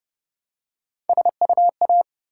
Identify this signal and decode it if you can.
Morse code, heard as a keyed tone, about 31 wpm, 700 Hz. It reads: SUA